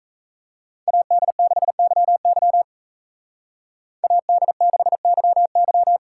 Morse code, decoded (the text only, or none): AD6YY AD6YY